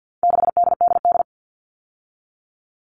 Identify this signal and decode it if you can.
Morse code, heard as a keyed tone, about 50 wpm, 700 Hz. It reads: 6DDD
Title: All Cut Numbers - 4-Digit 50wpm